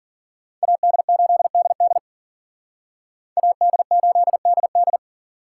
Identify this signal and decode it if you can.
Morse code, heard as a keyed tone, about 49 wpm, 700 Hz. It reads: AD8DD AD8DD
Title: Speed Racing - USA 2x2 Call Signs 40wpm